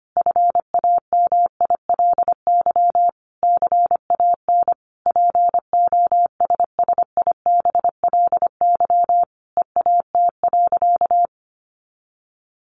Morse code, heard as a keyed tone, about 25 wpm, 700 Hz. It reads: FAMILY CAN POSSIBLY EAT.
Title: Sentences from Top 1300 Words 25wpm